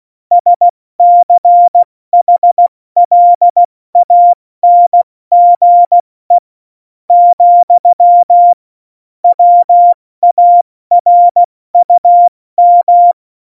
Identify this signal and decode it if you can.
Morse code, heard as a keyed tone, about 16 wpm, spaced as for 14 wpm, 700 Hz. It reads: SCHLANGE, WARUM
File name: Morsekurs / Witze / Witz 27 March 2018, 70 F80